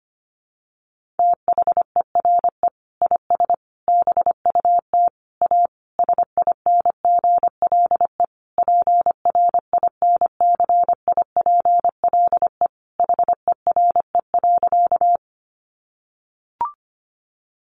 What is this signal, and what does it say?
Morse code, heard as a keyed tone, about 25 wpm, 700 Hz. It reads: THERE IS BUT A SINGLE PRINCIPLE HERE.
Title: Sentences from Top 1400 Words 25wpm